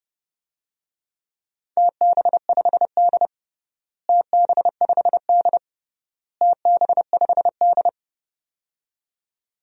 Morse code, heard as a keyed tone, about 30 wpm, 700 Hz. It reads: TB5D TB5D TB5D
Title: DXCC-100 Call Signs - 4 Characters - Repeated 3x - 30wpm